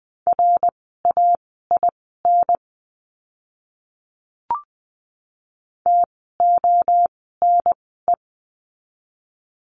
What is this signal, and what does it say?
Morse code, heard as a keyed tone, about 20 wpm, spaced as for 14 wpm, 700 Hz. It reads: RAIN TONE
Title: 068 Lesson 12f - Letter d - Words 20wpm